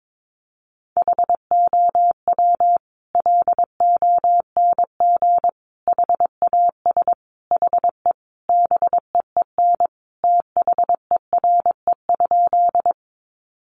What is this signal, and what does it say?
Morse code, heard as a keyed tone, about 22 wpm, 700 Hz. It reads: HOW LONG HAS HE BEEN THERE?
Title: Sentences from Top 100 Words 22wpm